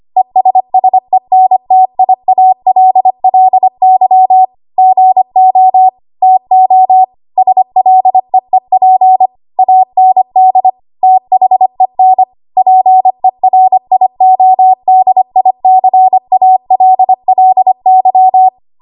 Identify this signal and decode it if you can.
Morse code, heard as a keyed tone, about 25 wpm, 750 Hz. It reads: ESSENTIALLY GO TO SLEEP AND THEN PERIODICALLY
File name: Archive / 25 WPM / 25WPM